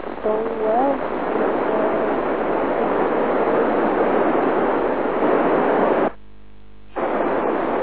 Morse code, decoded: none

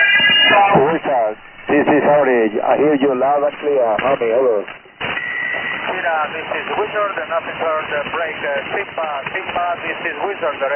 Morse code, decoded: none

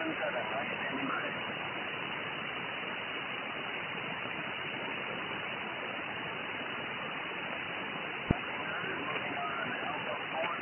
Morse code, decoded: none

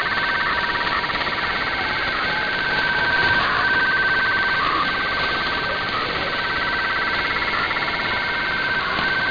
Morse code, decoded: none

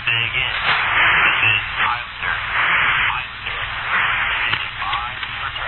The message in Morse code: none

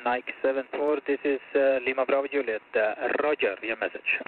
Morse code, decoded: none

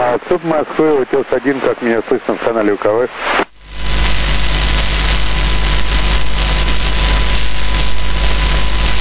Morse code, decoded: none

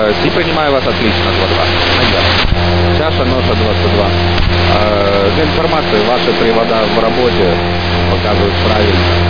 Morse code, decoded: none

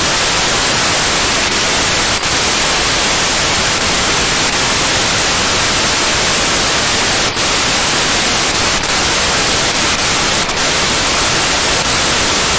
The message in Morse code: none

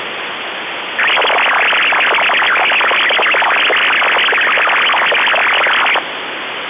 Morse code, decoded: none

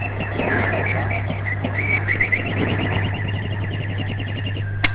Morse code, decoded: none

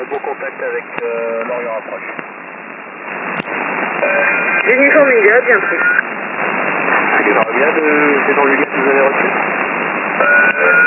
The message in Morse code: none

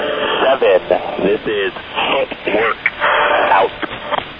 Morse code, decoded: none

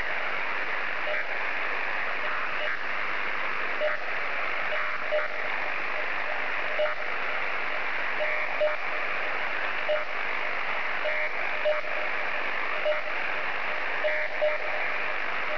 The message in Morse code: none

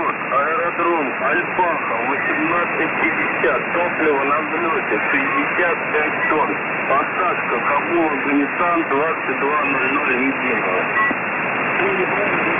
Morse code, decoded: none

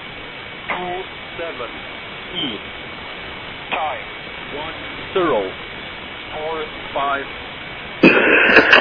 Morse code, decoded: none

